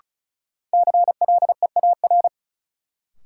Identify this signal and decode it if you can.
Morse code, heard as a keyed tone, about 36 wpm, 700 Hz. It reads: CLEAR
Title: Sets of 5 Words Encore - New Word Drill - 35wpm